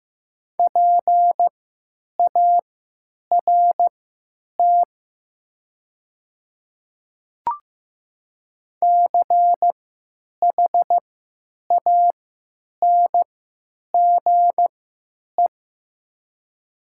Morse code, heard as a keyed tone, about 15 wpm, spaced as for 9 wpm, 700 Hz. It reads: PART CHANGE
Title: 3x Character Spacing - Top 200 Words - 15wpm